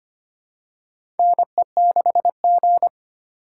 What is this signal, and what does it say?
Morse code, heard as a keyed tone, about 25 wpm, 700 Hz. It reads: NE6G
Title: USA Call Signs - 4 Characters - 3x Character Spacing - 25wpm